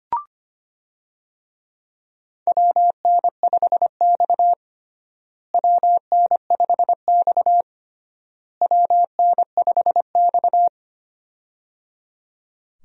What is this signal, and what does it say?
Morse code, heard as a keyed tone, about 25 wpm, 700 Hz. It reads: WN5X WN5X WN5X
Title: USA Call Signs - 4 Characters - Repeated 3x - 25wpm